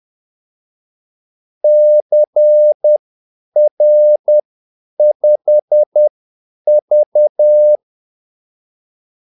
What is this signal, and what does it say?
Morse code, heard as a keyed tone, about 10 wpm, spaced as for 8 wpm, 600 Hz. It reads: CR5V
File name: Callsigns-10-8wpm